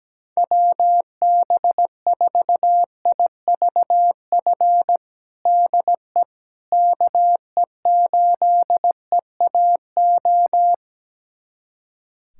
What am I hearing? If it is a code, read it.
Morse code, heard as a keyed tone, about 17 wpm, 700 Hz. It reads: WB4IVF DE KE8EAO
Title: USA Callsign DE Callsign 17wpm